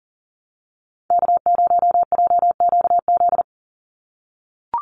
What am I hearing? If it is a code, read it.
Morse code, heard as a keyed tone, about 40 wpm, 700 Hz. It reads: K0JQZ
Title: NA SOTA Activator Callsigns 40wpm